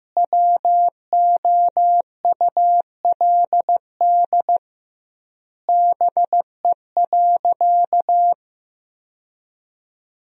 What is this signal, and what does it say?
Morse code, heard as a keyed tone, about 15 wpm, 700 Hz. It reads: WOULD BE.
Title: 2x Word-Spacing - Sentences from Top 200 Words - 15wpm